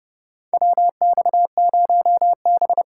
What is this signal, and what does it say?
Morse code, heard as a keyed tone, about 30 wpm, 700 Hz. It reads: WX0B
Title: USA Call Signs - 4 Characters - 2x Character Spacing - 30wpm